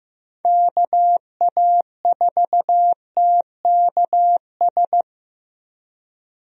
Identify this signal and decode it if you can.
Morse code, heard as a keyed tone, about 15 wpm, 700 Hz. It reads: KA4TKS
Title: USA Call Signs - 6 Characters - 15wpm